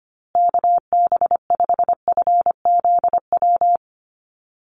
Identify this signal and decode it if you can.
Morse code, heard as a keyed tone, about 25 wpm, 700 Hz. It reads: KB5FZW
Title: USA Call Signs Repeated 3x 25wpm